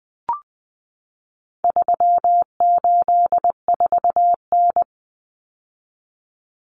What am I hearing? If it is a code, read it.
Morse code, heard as a keyed tone, about 20 wpm, 700 Hz. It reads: 384N
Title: Common Cut Numbers - 4-Digit 20wpm